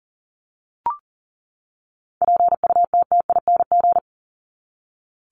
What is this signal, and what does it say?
Morse code, heard as a keyed tone, about 40 wpm, 700 Hz. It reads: PUTTING
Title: ING Suffix - Top 100 Words with Base Words 40wpm